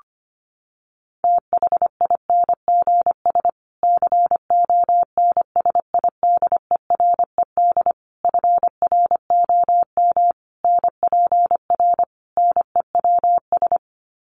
Morse code, heard as a keyed tone, about 25 wpm, 700 Hz. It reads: THINGS CONSIDERED FROM NPR NEWS
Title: Sets of 5 Words Encore 25wpm